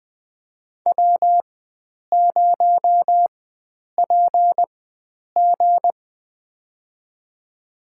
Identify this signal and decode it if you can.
Morse code, heard as a keyed tone, about 20 wpm, spaced as for 9 wpm, 700 Hz. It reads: W0PG
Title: USA Call Signs - 4 Characters - 4x Character Spacing - 20wpm